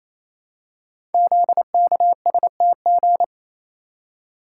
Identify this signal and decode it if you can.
Morse code, heard as a keyed tone, about 29 wpm, 700 Hz. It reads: ZKSTG
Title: Code Groups - 5 Letters - 28wpm